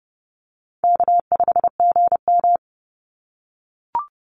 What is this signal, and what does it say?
Morse code, heard as a keyed tone, about 28 wpm, 700 Hz. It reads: K5GM
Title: Easy Call Signs - 30wpm